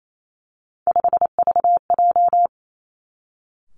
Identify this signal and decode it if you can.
Morse code, heard as a keyed tone, about 28 wpm, 700 Hz. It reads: 5VJ
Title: Three Challenging Letter-Number Combinations 28wpm